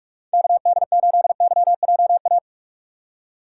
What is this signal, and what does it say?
Morse code, heard as a keyed tone, about 47 wpm, 700 Hz. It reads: KD8YJA
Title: USA Call Signs - 6 Characters - 6x Character Spacing - 45wpm